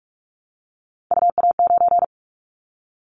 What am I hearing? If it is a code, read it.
Morse code, heard as a keyed tone, about 46 wpm, 700 Hz. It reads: UA9
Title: Callsign Prefixes for Top 50 DXCC - 45wpm